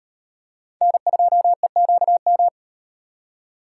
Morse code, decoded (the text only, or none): N2EQM